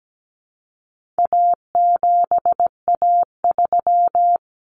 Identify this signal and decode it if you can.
Morse code, heard as a keyed tone, about 17 wpm, 700 Hz. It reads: A7A3
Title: Common Cut Numbers - 4-Digit 17wpm